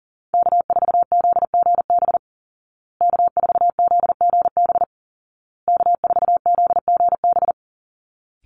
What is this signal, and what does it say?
Morse code, heard as a keyed tone, about 40 wpm, 700 Hz. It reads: K4ZGB K4ZGB K4ZGB
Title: Call Signs Repeated 2x - 40wpm